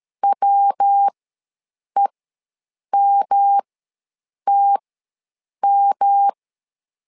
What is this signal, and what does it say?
Morse code, heard as a keyed tone, about 13 wpm, spaced as for 7 wpm, 800 Hz. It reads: WEMTM
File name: LEZIONI / 3 MWRO / 3 MW ws 12wpm 1816